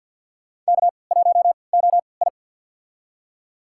Morse code, heard as a keyed tone, about 51 wpm, spaced as for 28 wpm, 700 Hz. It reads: K1OI
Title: USA Call Signs - 4 Characters - 3x Character Spacing - 50wpm